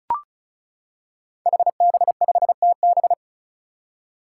Morse code, heard as a keyed tone, about 36 wpm, 700 Hz. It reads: HB5TB
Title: Five Challenging Letter-Number Combinations 35wpm